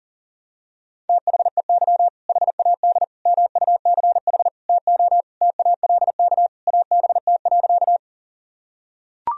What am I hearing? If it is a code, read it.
Morse code, heard as a keyed tone, about 37 wpm, 700 Hz. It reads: THEY HAD MUCH TO TALK ABT.
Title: Sentences from Top 400 Words and Abbreviations 40wpm